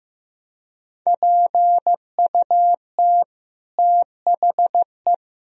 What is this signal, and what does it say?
Morse code, heard as a keyed tone, about 15 wpm, 700 Hz. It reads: PUT THE
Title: Sentences from Top 1000 Words - Review Part 3 - 15wpm